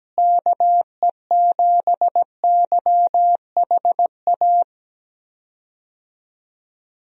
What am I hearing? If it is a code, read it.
Morse code, heard as a keyed tone, about 17 wpm, 700 Hz. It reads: KE7YHA
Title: USA Call Signs - 6 Characters - Repeated 3x - 17wpm